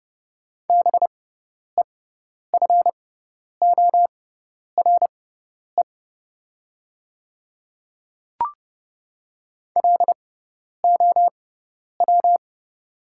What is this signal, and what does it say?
Morse code, heard as a keyed tone, about 30 wpm, spaced as for 10 wpm, 700 Hz. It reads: BEFORE LOW
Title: 6x Character Spacing - Top 200 Words - 30wpm